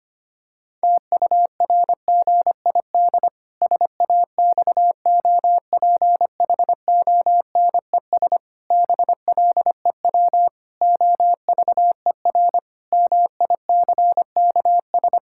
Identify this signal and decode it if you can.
Morse code, heard as a keyed tone, about 25 wpm, 700 Hz. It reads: TURGID SAXOPHONES BLEW OVER MICKS
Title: Pangram Challenge v2 - 25wpm